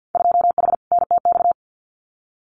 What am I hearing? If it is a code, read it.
Morse code, heard as a keyed tone, about 50 wpm, 700 Hz. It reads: 2H NTX
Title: Winter Field Day Exchanges Encore 50wpm